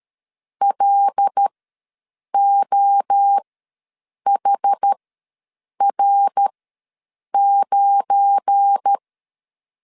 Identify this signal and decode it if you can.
Morse code, heard as a keyed tone, about 13 wpm, spaced as for 7 wpm, 800 Hz. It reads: LOHR9
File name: LEZIONI / 7 GF9Z / 7 GF9 ws 12wpm 3344